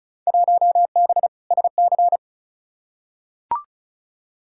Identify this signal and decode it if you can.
Morse code, heard as a keyed tone, about 34 wpm, 700 Hz. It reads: 1B SC
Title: Field Day Exchanges - 35wpm